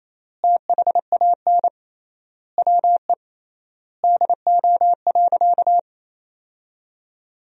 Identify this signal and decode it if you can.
Morse code, heard as a keyed tone, about 28 wpm, 700 Hz. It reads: THAN WE DO.
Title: 3x Word-Spacing - Sentences from Top 100 Words - 28wpm